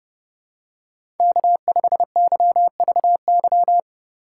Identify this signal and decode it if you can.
Morse code, heard as a keyed tone, about 30 wpm, 700 Hz. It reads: K5YVY
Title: K1USN SST QSO Practice - 30wpm